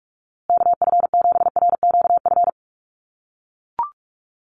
Morse code, heard as a keyed tone, about 44 wpm, 700 Hz. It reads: KF7RQF
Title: USA Call Signs Repeated 2x 45wpm